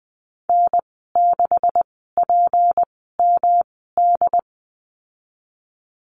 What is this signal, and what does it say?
Morse code, heard as a keyed tone, about 20 wpm, spaced as for 15 wpm, 700 Hz. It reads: N6PMD
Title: 154 Lesson 25g - Letter p - Call Signs 20wpm